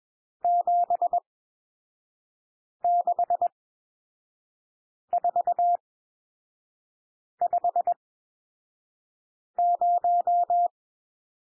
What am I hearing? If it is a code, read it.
Morse code, heard as a keyed tone, about 21 wpm, spaced as for 5 wpm, 700 Hz. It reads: 76450
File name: cwmp3 / N1 SOUND NUMBERS 21 wpm